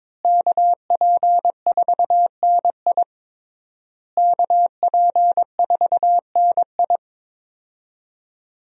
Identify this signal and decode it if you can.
Morse code, heard as a keyed tone, about 22 wpm, 700 Hz. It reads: KP4NI KP4NI
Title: USA Call Signs - 5 Characters - Repeated 3x - 22wpm